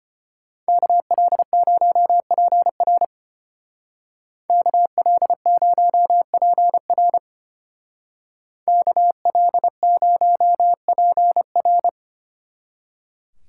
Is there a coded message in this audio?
Yes: Morse code from 0.7 to 11.9 s, heard as a keyed tone, about 33 wpm, 700 Hz. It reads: KL0PR KL0PR KL0PR